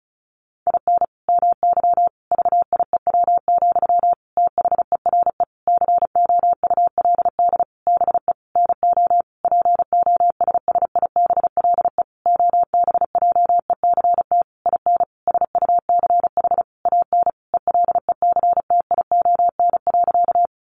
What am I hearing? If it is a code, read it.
Morse code, heard as a keyed tone, about 34 wpm, 700 Hz. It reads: IN MY VIEW, THERE COULD BE NO POSSIBLE OBJECT IN SUCH AN ELECTION.
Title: Sentences from Top 1500 Words 35wpm